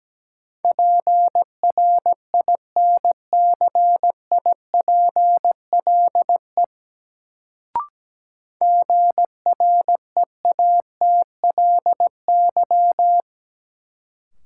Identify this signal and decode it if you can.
Morse code, heard as a keyed tone, about 17 wpm, 700 Hz. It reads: PRINCIPLE GREATLY
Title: Sentences from Top 1400 Words - New Word Drill - 17wpm